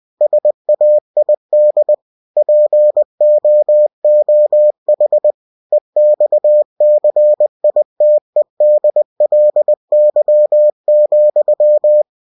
Morse code, heard as a keyed tone, about 20 wpm, 600 Hz. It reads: SAID POOH EXCITEDLY,
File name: cw-books / winnie-the-pooh / winnie-20wpm-page-0036